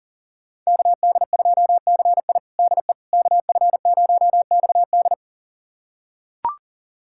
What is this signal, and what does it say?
Morse code, heard as a keyed tone, about 37 wpm, 700 Hz. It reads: KD2CI DE KF0XD
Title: USA Callsign DE Callsign - 5 Characters - 40wpm